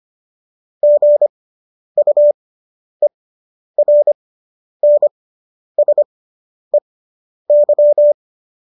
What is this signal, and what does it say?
Morse code, heard as a keyed tone, about 25 wpm, spaced as for 10 wpm, 590 Hz. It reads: GUERNSEY